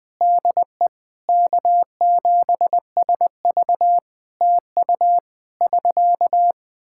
Morse code, heard as a keyed tone, about 20 wpm, 700 Hz. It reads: DE K7SV TU <SK>
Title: Practice QSO - Exchange 4 - Ending - 20wpm